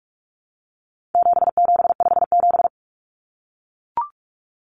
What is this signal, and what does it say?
Morse code, heard as a keyed tone, about 44 wpm, 700 Hz. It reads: 7757
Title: Numbers - 4-Digit 45wpm